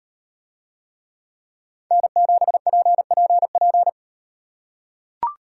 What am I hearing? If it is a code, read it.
Morse code, heard as a keyed tone, about 38 wpm, 700 Hz. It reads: N7PPP